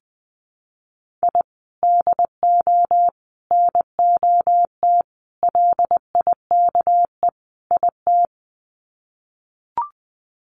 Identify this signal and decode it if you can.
Morse code, heard as a keyed tone, about 20 wpm, 700 Hz. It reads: I DO NOT LIKE IT
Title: Sets of 5 Words Encore 20wpm